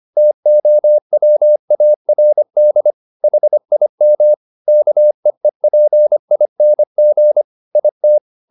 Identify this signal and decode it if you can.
Morse code, heard as a keyed tone, about 25 wpm, 600 Hz. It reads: TOWARD HIM KEEPING IT
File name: the old man and the sea part 1 25wpm